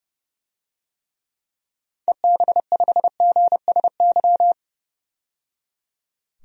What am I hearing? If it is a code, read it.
Morse code, heard as a keyed tone, about 29 wpm, 700 Hz. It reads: EB5GSY
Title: DXCC-100 Call Signs - 6 Characters - 30wpm